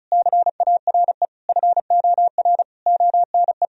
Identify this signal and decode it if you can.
Morse code, heard as a keyed tone, about 36 wpm, 700 Hz. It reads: CARE FOR ONE
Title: Sentences from Top 1000 Words - Review Part 1 - 35wpm